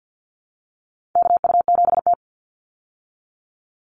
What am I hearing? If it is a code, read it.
Morse code, heard as a keyed tone, about 50 wpm, 700 Hz. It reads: KU7T
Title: Call Signs Repeated 3x - 50wpm